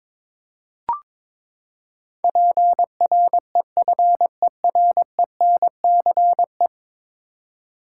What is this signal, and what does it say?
Morse code, heard as a keyed tone, about 22 wpm, 700 Hz. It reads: PREFERENCE
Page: morse-code-ninja-practice-22wpm